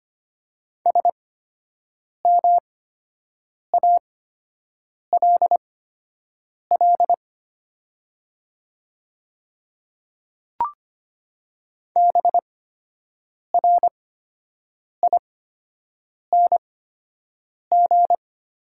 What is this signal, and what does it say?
Morse code, heard as a keyed tone, about 25 wpm, spaced as for 7 wpm, 700 Hz. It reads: SMALL BRING
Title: 8x Character Spacing - Top 500 Words - 25wpm